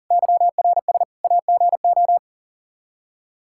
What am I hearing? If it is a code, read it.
Morse code, heard as a keyed tone, about 40 wpm, 700 Hz. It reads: YRS AGO
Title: Practice QSO - Exchange 3 - More Info - 40wpm